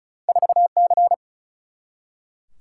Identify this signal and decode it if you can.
Morse code, heard as a keyed tone, about 36 wpm, 700 Hz. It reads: 4 C
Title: Single-Letter Single-Number 35wpm